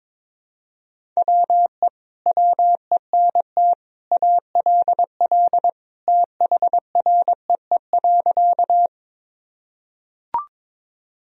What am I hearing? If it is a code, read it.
Morse code, heard as a keyed tone, about 22 wpm, 700 Hz. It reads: WE WENT ALL THREE.
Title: Sentences from Top 1000 Words - Review Part 1 - 22wpm